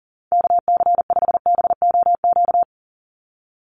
Morse code, heard as a keyed tone, about 40 wpm, 700 Hz. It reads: KC5BOQ
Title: USA Call Signs 40wpm